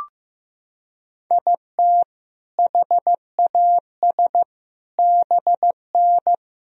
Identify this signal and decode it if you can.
Morse code, heard as a keyed tone, about 16 wpm, 700 Hz. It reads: IT HAS BN
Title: Sentences from Top 400 Words and Abbreviations 15wpm